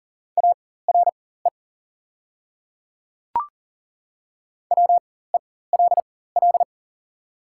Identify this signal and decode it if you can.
Morse code, heard as a keyed tone, about 37 wpm, spaced as for 18 wpm, 690 Hz. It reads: ARE WELL
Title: 4x Character Spacing - Top 200 Words - 40wpm